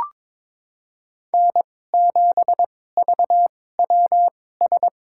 Morse code, heard as a keyed tone, about 22 wpm, spaced as for 16 wpm, 700 Hz. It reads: N7VWS
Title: USA Call Signs - 5 Characters - 2x Character Spacing - 22wpm